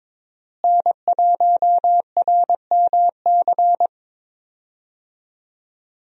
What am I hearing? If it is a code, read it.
Morse code, heard as a keyed tone, about 22 wpm, 700 Hz. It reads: N1RMC
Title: USA Call Signs - 5 Characters - Repeated 3x - 22wpm